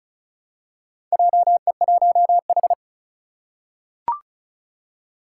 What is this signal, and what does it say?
Morse code, heard as a keyed tone, about 36 wpm, 700 Hz. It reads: JE1H
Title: Four Challenging Letter-Number Combinations 35wpm